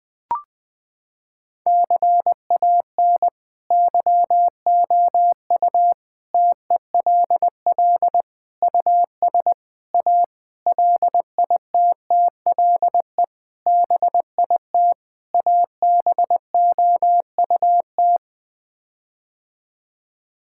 Morse code, heard as a keyed tone, about 20 wpm, 700 Hz. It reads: CAN YOU TELL US A LITTLE BIT ABOUT
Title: Sets of 8 Words - 20wpm